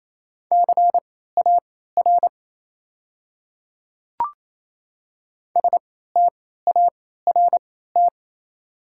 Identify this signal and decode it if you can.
Morse code, heard as a keyed tone, about 28 wpm, spaced as for 16 wpm, 690 Hz. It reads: CAR START
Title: 3x Character Spacing - Top 500 Words - 28wpm